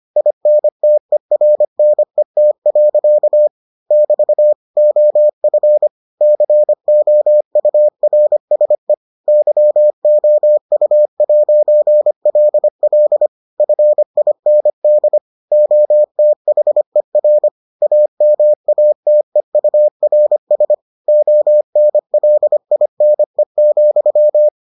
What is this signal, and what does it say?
Morse code, heard as a keyed tone, about 25 wpm, 600 Hz. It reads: INTERNET. <BT> OF COURSE YOU'LL FIND OTHER AMATEURS ONLINE,